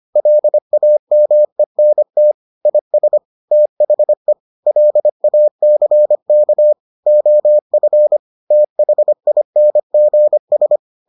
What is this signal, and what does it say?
Morse code, heard as a keyed tone, about 25 wpm, 600 Hz. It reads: LAMENT IS THE LACK OF THINGS